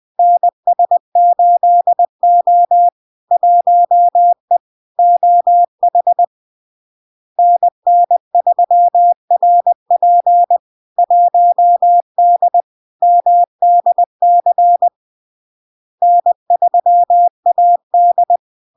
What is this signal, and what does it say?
Morse code, heard as a keyed tone, about 20 wpm, 700 Hz. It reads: NS8O 1E OH NN3RP 1D MDC N3AD
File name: LARK-FD-2022-Log-in-CW-20wpm-700-Hz